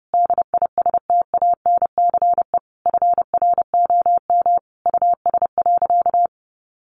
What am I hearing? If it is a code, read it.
Morse code, heard as a keyed tone, about 30 wpm, 700 Hz. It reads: DISTANCE FROM US.
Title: Sentences from Top 1200 Words 30wpm